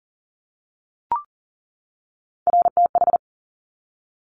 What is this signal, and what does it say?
Morse code, heard as a keyed tone, about 38 wpm, 700 Hz. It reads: RTH
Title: Top Three Letter Combinations in Words 40wpm